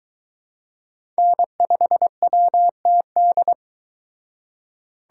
Morse code, heard as a keyed tone, about 22 wpm, 700 Hz. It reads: N5WTD